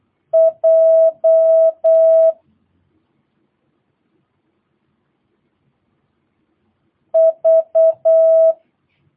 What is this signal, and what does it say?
Morse code, heard as a keyed tone, about 8 wpm, 650 Hz. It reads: J V
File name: jh8cbh / 2022cw